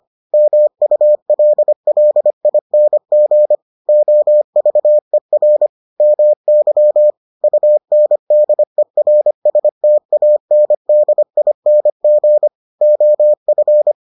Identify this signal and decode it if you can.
Morse code, heard as a keyed tone, about 25 wpm, 600 Hz. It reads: MULLING OVER MY UNDERSTANDING OF